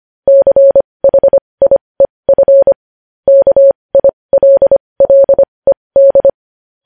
Morse code, heard as a keyed tone, about 25 wpm, spaced as for 20 wpm, 550 Hz. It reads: CHIEF KILLED
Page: News headlines in Morse code at 20 wpm